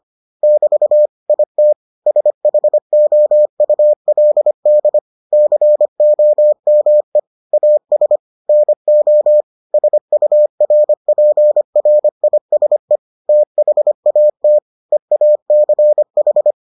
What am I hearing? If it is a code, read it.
Morse code, heard as a keyed tone, about 25 wpm, 600 Hz. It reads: <BT> IT SHOULD COME AS NO SURPRISE THAT EACH